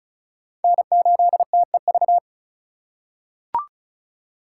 Morse code, heard as a keyed tone, about 36 wpm, 700 Hz. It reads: N8TEV
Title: USA Call Signs - 5 Characters - 3x Character Spacing - 35wpm